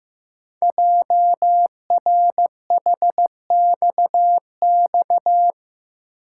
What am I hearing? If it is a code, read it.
Morse code, heard as a keyed tone, about 15 wpm, 690 Hz. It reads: JRHXX
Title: Code Groups - 5 Letters - 15wpm